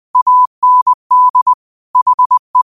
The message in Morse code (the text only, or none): AND HE